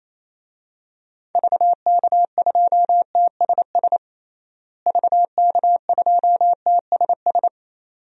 Morse code, heard as a keyed tone, about 27 wpm, 700 Hz. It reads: VK2TSS VK2TSS
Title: DXCC-100 Call Signs - 6 Characters - Repeated 2x - 28wpm